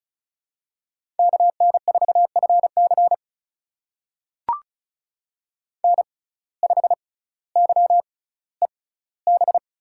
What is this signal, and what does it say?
Morse code, heard as a keyed tone, about 36 wpm, spaced as for 18 wpm, 700 Hz. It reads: KN4FC N5YEB
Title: USA Call Signs - 5 Characters - 6x Character Spacing - 35wpm